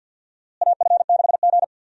Morse code, heard as a keyed tone, about 46 wpm, 700 Hz. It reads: AF6G
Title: USA Call Signs - 4 Characters - Repeated 2x - 50wpm